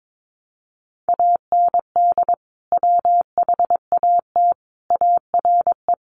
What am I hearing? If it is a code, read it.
Morse code, heard as a keyed tone, about 22 wpm, 700 Hz. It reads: AND WHAT ARE
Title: Sentences from Top 200 Words 22wpm